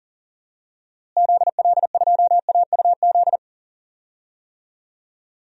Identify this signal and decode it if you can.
Morse code, heard as a keyed tone, about 40 wpm, 700 Hz. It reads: ZL2AUZ
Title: DXCC-100 Call Signs - 6 Characters - Repeated 3x - 40wpm